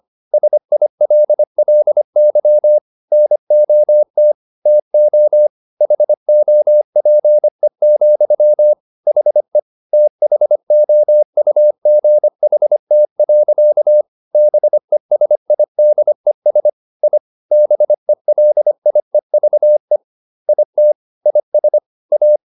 Morse code, heard as a keyed tone, about 25 wpm, 600 Hz. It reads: SILLY NOT TO HOPE, HE THOUGHT. BESIDES I BELIEVE IT IS A